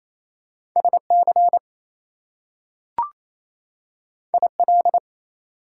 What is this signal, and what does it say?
Morse code, heard as a keyed tone, about 28 wpm, 700 Hz. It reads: SC IL
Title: US States and Canadian Provinces 28wpm